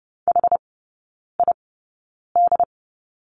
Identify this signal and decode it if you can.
Morse code, heard as a keyed tone, about 30 wpm, spaced as for 9 wpm, 700 Hz. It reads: HID
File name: 081 Lesson 14e - Number 2 - Three Characters 30wpm